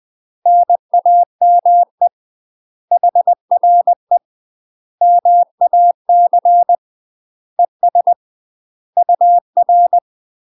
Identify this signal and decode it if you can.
Morse code, heard as a keyed tone, about 20 wpm, 700 Hz. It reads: NAME HRE MAC ES UR
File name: UK MASTER QSOs TRAINING / 20WPM / UK MASTER QSOs 0031